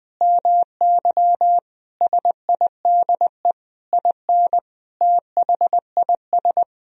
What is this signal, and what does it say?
Morse code, heard as a keyed tone, about 20 wpm, 700 Hz. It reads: MY SIDE IN THIS
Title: Sentences from Top 1000 Words - Review Part 1 - 20wpm